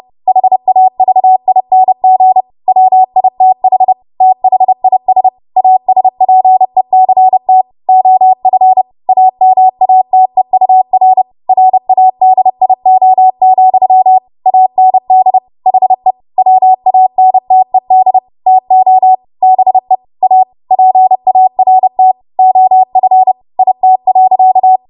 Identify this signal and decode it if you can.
Morse code, heard as a keyed tone, about 30 wpm, 750 Hz. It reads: HAVING WITH THIS ASPECT OF AMATEUR RADIO, AND HE WANTED TO BE A PART OF IT.
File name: Morse / 30WPM